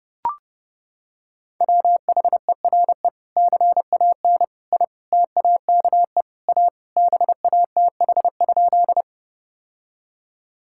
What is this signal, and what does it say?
Morse code, heard as a keyed tone, about 28 wpm, 700 Hz. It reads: WHERE CAN I TAKE A BATH?
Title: Sentences from Top 1600 Words 30wpm